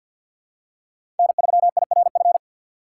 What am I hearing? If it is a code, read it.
Morse code, heard as a keyed tone, about 52 wpm, 700 Hz. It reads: N3IRF
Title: USA Call Signs - 5 Characters - 3x Character Spacing - 50wpm